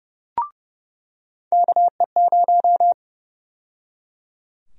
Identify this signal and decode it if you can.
Morse code, heard as a keyed tone, about 31 wpm, 700 Hz. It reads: KE0
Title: USA Call Sign Prefixes - 30wpm